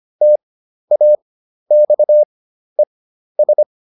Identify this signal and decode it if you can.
Morse code, heard as a keyed tone, about 25 wpm, spaced as for 12 wpm, 600 Hz. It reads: TAXES